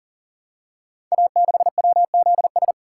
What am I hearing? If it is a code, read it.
Morse code, heard as a keyed tone, about 39 wpm, 700 Hz. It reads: A6WZS
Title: Code Groups - 5 Letter:Numbers - 40wpm